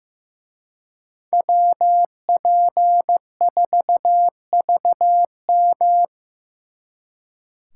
Morse code, heard as a keyed tone, about 18 wpm, 700 Hz. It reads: WP4VM
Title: Speed Racing - USA 2x2 Call Signs 15wpm